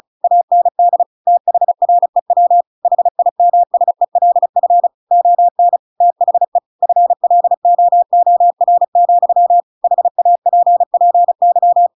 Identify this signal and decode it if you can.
Morse code, heard as a keyed tone, about 34 wpm, 700 Hz. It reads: AND THREW HIMSELF ON THE FLOOR, HAPPY